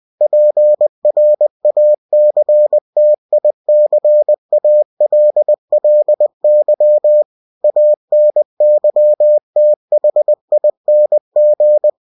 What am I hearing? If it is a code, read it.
Morse code, heard as a keyed tone, about 20 wpm, 600 Hz. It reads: PRACTICALLY ANYTHING
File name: cw-books / winnie-the-pooh / winnie-20wpm-page-0112